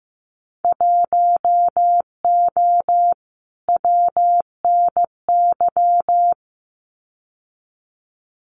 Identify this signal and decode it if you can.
Morse code, heard as a keyed tone, about 15 wpm, 700 Hz. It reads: 1O WNY
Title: Winter Field Day Exchanges 15wpm